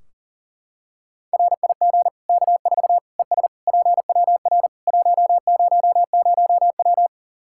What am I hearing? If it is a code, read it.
Morse code, heard as a keyed tone, about 40 wpm, 700 Hz. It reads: RIG K4 ES PWR 100W
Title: K1USN SST QSO Practice 40wpm